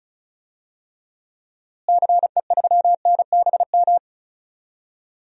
Morse code, heard as a keyed tone, about 36 wpm, 700 Hz. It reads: CE3NBM
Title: DXCC-100 Call Signs - 6 Characters - Repeated 3x - 35wpm